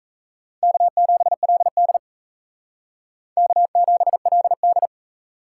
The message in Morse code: K7LD K7LD

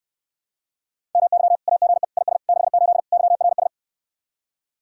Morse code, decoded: MY AGE IS 68 YRS

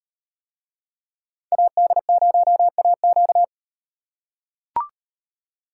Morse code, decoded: AD0AQ

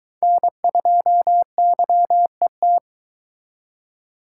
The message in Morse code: N2YET

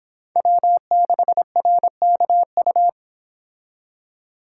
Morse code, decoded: W6RKU

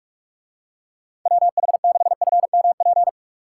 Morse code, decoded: WH6FMP

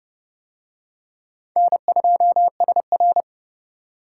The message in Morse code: N2SR

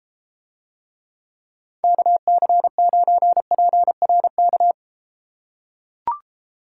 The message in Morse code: KC9PRK

